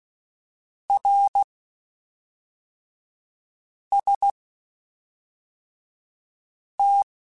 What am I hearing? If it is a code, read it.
Morse code, heard as a keyed tone, about 16 wpm, spaced as for 3 wpm, 800 Hz. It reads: RST
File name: (spaced) / QSOs Spaced-mp3s / kc7haa @5 sp